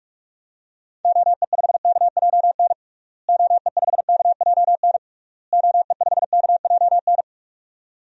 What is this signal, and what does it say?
Morse code, heard as a keyed tone, about 46 wpm, 700 Hz. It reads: OE5KJN OE5KJN OE5KJN
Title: DXCC-100 Call Signs - 6 Characters - Repeated 3x - 45wpm